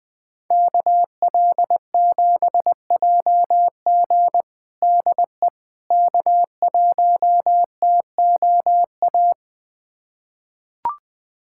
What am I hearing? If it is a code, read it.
Morse code, heard as a keyed tone, about 20 wpm, 700 Hz. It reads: KL7JG DE K1TOA
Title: USA Callsign DE Callsign - 5 Characters - 20wpm